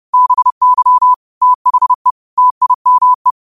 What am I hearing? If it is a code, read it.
Morse code, heard as a keyed tone, about 30 wpm, 1000 Hz. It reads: BY THE TIME